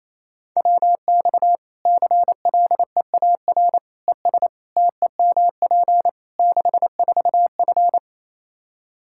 Morse code, heard as a keyed tone, about 28 wpm, 700 Hz. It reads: WX CLEAR ES TEMP 64F
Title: K1USN SST QSO Practice 28wpm